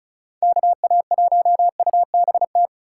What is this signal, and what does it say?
Morse code, heard as a keyed tone, about 43 wpm, 700 Hz. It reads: KA1UBT
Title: Speed Racing - USA 2x3 Call Signs 35wpm